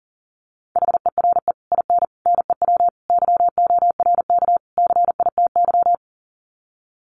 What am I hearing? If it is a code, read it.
Morse code, heard as a keyed tone, about 40 wpm, 700 Hz. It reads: HERE IN NEW YORK CITY
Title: Sets of 5 Words Encore2 40wpm